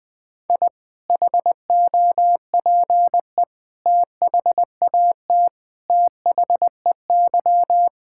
Morse code, heard as a keyed tone, about 20 wpm, 700 Hz. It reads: I HOPE THAT THEY
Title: Sentences with 3-Word Phrases - Spoken Beginning - 20wpm